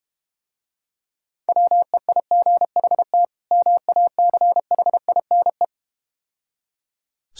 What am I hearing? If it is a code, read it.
Morse code, heard as a keyed tone, about 32 wpm, 700 Hz. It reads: WEIGHT MACHINE